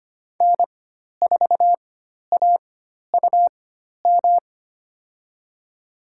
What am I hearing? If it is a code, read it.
Morse code, heard as a keyed tone, about 25 wpm, spaced as for 12 wpm, 700 Hz. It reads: N4AUM
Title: USA Call Signs - 5 Characters - 4x Character Spacing - 25wpm